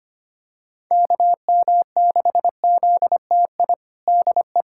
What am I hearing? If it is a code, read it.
Morse code, heard as a keyed tone, about 25 wpm, 700 Hz. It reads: KM6ZTI DE
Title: USA Callsign DE Callsign - 6 Characters - 25wpm